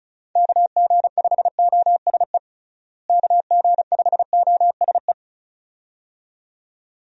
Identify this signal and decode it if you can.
Morse code, heard as a keyed tone, about 34 wpm, 700 Hz. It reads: KG5OSE KG5OSE
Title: USA Call Signs - 6 Characters - Repeated 2x - 35wpm